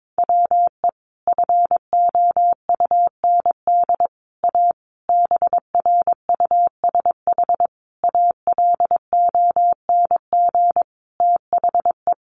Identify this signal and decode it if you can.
Morse code, heard as a keyed tone, about 22 wpm, 700 Hz. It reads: WE FOUND A BRUSH ALONG THE
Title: Sentences from Top 1100 Words 22wpm